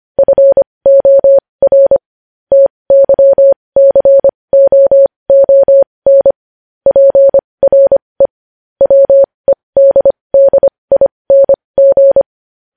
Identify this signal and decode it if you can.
Morse code, heard as a keyed tone, about 25 wpm, spaced as for 20 wpm, 550 Hz. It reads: FOR TYCOON PRE WEDDING
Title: News Headlines at 20